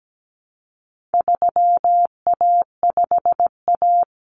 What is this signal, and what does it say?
Morse code, heard as a keyed tone, about 17 wpm, 700 Hz. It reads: 3A5A
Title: Common Cut Numbers - 4-Digit 17wpm